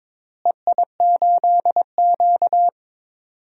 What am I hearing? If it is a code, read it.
Morse code, heard as a keyed tone, about 22 wpm, 700 Hz. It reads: EI8Q
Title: DXCC-100 Call Signs - 4 Characters - Repeated 2x - 22wpm